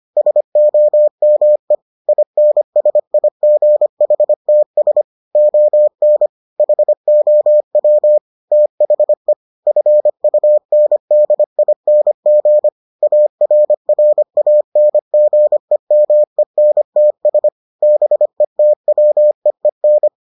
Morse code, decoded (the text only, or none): SOME INSIGHTS ON HOW THE FUNDING ARRANGEMENTS BETWEEN